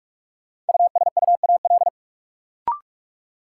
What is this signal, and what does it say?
Morse code, heard as a keyed tone, about 44 wpm, 700 Hz. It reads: USUAL
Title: Sentences from Top 400 Words - New Word Drill - 45wpm